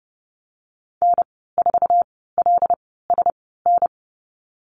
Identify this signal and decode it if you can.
Morse code, heard as a keyed tone, about 30 wpm, spaced as for 17 wpm, 700 Hz. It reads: N4LSN
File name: Lesson 15g - Number 5 - Call Signs 30wpm